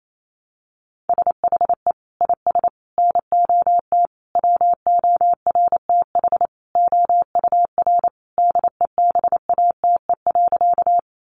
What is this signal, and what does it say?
Morse code, heard as a keyed tone, about 28 wpm, 700 Hz. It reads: SHE IS NOT WORTH OUR DEBATE.
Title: Sentences from Top 1100 Words 28wpm